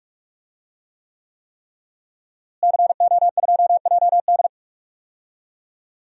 Morse code, heard as a keyed tone, about 46 wpm, 700 Hz. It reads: CO2JD
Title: DXCC-100 Call Signs - 5 Characters - Repeated 3x - 45wpm